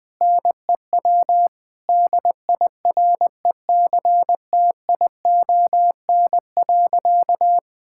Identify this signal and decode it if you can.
Morse code, heard as a keyed tone, about 20 wpm, 700 Hz. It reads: NEW DIRECTION.
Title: Sentences from Top 1000 Words - Review Part 5 - 20wpm